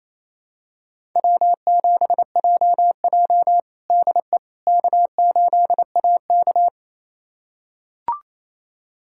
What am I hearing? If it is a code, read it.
Morse code, heard as a keyed tone, about 28 wpm, 700 Hz. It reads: W7JJ DE K8AK
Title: USA Callsign DE Callsign - 4 Characters - 28wpm